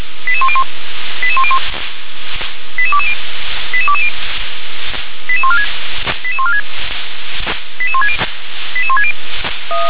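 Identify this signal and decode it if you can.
Five-tone selective call (ZVEI), heard as quick runs of five tones, tones of about 70 ms. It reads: 88101 88101 88220 88220 88146 88146 88160 88160
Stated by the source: ZVEI five-tone selective call 88051, received by SDR.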